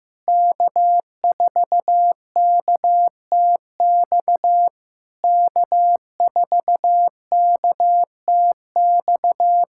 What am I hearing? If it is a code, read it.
Morse code, heard as a keyed tone, about 15 wpm, 700 Hz. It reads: K4KTX K4KTX
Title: SOTA QSO Practice 15wpm